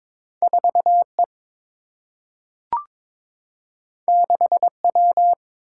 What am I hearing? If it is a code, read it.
Morse code, heard as a keyed tone, about 22 wpm, 700 Hz. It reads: 4E 6W
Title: Single-Letter Single-Number 22wpm